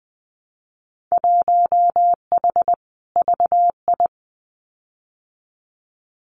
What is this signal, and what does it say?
Morse code, heard as a keyed tone, about 20 wpm, 700 Hz. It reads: 1H VI
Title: Winter Field Day Exchanges Encore 20wpm